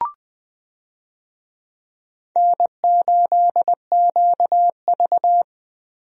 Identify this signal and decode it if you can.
Morse code, heard as a keyed tone, about 19 wpm, 700 Hz. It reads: N8QV
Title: USA Call Signs - 4 Characters - Repeated 2x - 20wpm